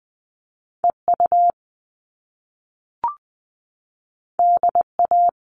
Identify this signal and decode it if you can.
Morse code, heard as a keyed tone, about 20 wpm, 700 Hz. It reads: EU DA
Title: All Cut Numbers - 2-Digit 20wpm